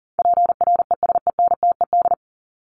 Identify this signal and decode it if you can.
Morse code, heard as a keyed tone, about 40 wpm, 700 Hz. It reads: PRESENTED
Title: ED Suffix - Top 100 Words with Base Words 40wpm